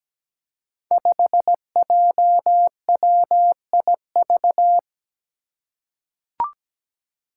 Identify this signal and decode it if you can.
Morse code, heard as a keyed tone, about 17 wpm, 700 Hz. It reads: 5JWIV
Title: Five Challenging Letter-Number Combinations 17wpm